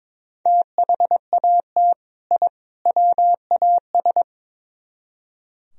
Morse code, heard as a keyed tone, about 24 wpm, 700 Hz. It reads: THAT I WAS
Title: Speed-Racing - Sets of 3 Words 22wpm